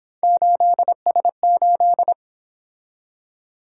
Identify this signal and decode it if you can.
Morse code, heard as a keyed tone, about 26 wpm, 700 Hz. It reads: 8S8
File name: Speed-Racing - Three Letter-Number Combinations - 22wpm